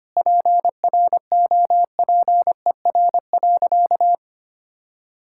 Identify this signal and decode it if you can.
Morse code, heard as a keyed tone, about 25 wpm, 700 Hz. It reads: PROPER.
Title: Sentences from Top 1000 Words - Review Part 5 - 25wpm